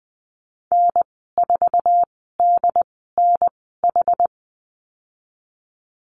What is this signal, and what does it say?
Morse code, heard as a keyed tone, about 20 wpm, spaced as for 14 wpm, 700 Hz. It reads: N4DNH
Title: 069 Lesson 12g - Letter d - Call Signs 20wpm